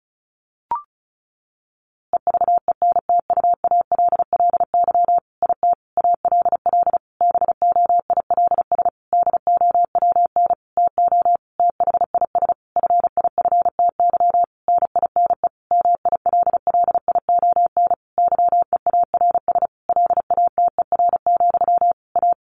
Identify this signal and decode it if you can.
Morse code, heard as a keyed tone, about 34 wpm, 700 Hz. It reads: EVENTUALLY IT ALL BOILS DOWN TO THIS FIFTY NINE MILLION YEARS LATER, A
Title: A Sound Of Thunder 35wpm